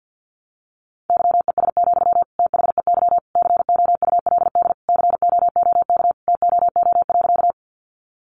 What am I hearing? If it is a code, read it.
Morse code, heard as a keyed tone, about 50 wpm, 700 Hz. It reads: YES, THEY COULD COOK TOO.
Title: Sentences from Top 700 Words 50wpm